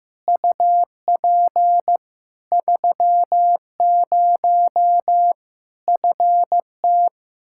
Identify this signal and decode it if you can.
Morse code, heard as a keyed tone, about 15 wpm, 700 Hz. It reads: UP 30 FT
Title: K1USN SST QSO Practice 15wpm